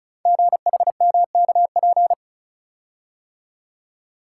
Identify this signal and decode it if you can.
Morse code, heard as a keyed tone, about 36 wpm, 700 Hz. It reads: GHMKP
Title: Code Groups - 5 Letters - 35wpm